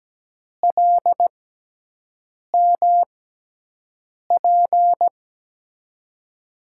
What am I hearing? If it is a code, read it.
Morse code, heard as a keyed tone, about 17 wpm, spaced as for 6 wpm, 700 Hz. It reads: LMP
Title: USA Call Signs - 6 Characters - 6x Character Spacing - 17wpm